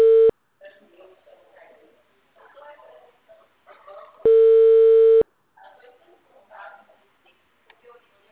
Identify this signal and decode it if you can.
Ringback tone (China, 450 Hz).
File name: out- -2003- - - .122